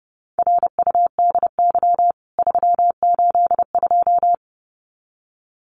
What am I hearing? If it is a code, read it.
Morse code, heard as a keyed tone, about 30 wpm, 700 Hz. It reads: RUDY 382
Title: CWT Exchange 30wpm